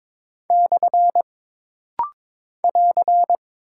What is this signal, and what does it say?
Morse code, heard as a keyed tone, about 22 wpm, 700 Hz. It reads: / <AR>
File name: practice / 22wpm / Single Punctuation-Prosign - Rapid-Fire - 22wpm